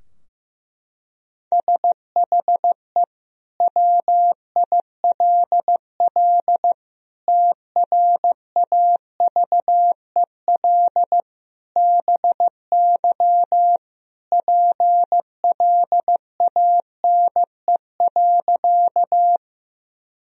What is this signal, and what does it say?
Morse code, heard as a keyed tone, about 15 wpm, 700 Hz. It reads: SHE WILL TRAVEL BY PLANE.
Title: Sentences from Top 1000 Words - Review Part 3 - 15wpm